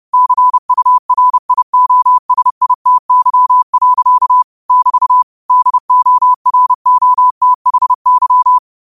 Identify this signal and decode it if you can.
Morse code, heard as a keyed tone, about 30 wpm, 1000 Hz. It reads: CURIOSITY. <BT> DOROTHY